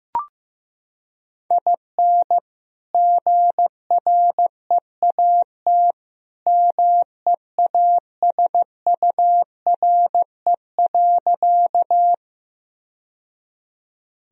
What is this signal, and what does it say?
Morse code, heard as a keyed tone, about 16 wpm, 700 Hz. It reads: IN GREAT MEASURE.
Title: Sentences from Top 1000 Words - Review Part 2 - 15wpm